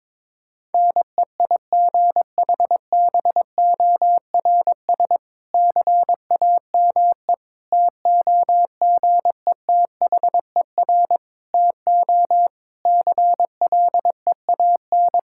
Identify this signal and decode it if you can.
Morse code, heard as a keyed tone, about 22 wpm, 700 Hz. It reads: NEIGHBORS CAME TOGETHER TO CLEAN
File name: Difficult Sentences with Hints - 22wpm